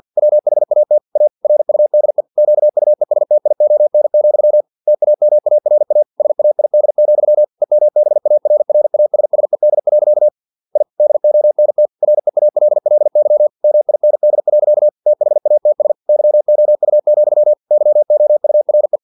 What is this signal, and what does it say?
Morse code, heard as a keyed tone, about 50 wpm, 600 Hz. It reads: WHAT A RUDE QUESTION, TAMARA SAID, EMBARRASSED. I DONT REALLY MIND. THATS YOU, YOURE
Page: Never 49wpm